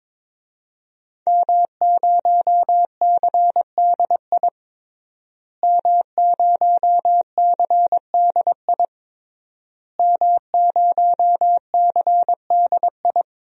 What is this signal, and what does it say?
Morse code, heard as a keyed tone, about 22 wpm, 700 Hz. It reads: M0CDI M0CDI M0CDI
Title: DXCC-100 Call Signs - 5 Characters - Repeated 3x - 22wpm